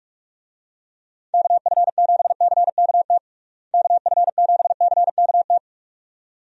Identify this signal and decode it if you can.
Morse code, heard as a keyed tone, about 45 wpm, 700 Hz. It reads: KF7CKT KF7CKT
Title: USA Call Signs - 6 Characters - Repeated 2x - 45wpm